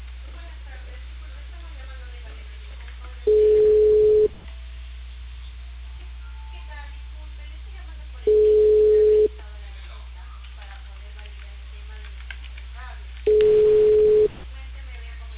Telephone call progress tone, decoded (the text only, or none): ringback tone